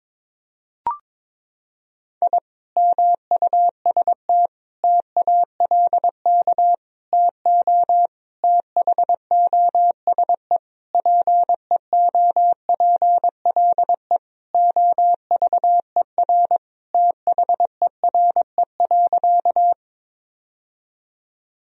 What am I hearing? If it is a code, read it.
Morse code, heard as a keyed tone, about 22 wpm, 700 Hz. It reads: I MUST TALK TO THOSE PEOPLE OVER THERE.
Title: Sentences from Top 1000 Words - Review Part 1 - 22wpm